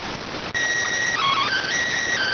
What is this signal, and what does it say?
SSTV (slow-scan television), PD 290.